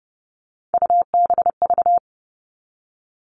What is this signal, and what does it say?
Morse code, heard as a keyed tone, about 30 wpm, 700 Hz. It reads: UBV
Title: All Cut Numbers - 3-Digit 30wpm